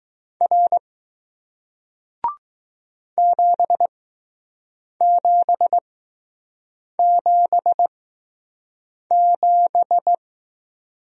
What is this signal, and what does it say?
Morse code, heard as a keyed tone, about 20 wpm, spaced as for 6 wpm, 700 Hz. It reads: R 7777